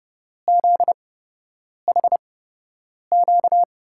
Morse code, heard as a keyed tone, about 30 wpm, spaced as for 8 wpm, 700 Hz. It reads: ZHQ
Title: USA Call Signs - 5 Characters - 8x Character Spacing - 30wpm